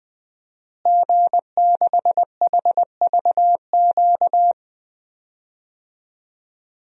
Morse code, heard as a keyed tone, about 21 wpm, 700 Hz. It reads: G6HVQ
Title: DXCC-100 Call Signs - 5 Characters - Repeated 3x - 20wpm